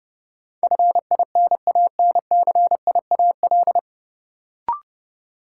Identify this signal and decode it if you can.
Morse code, heard as a keyed tone, about 31 wpm, 700 Hz. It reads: FINANCIAL
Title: Sentences from Top 1000 Words - Review Part 5 - New Word Drill - 30wpm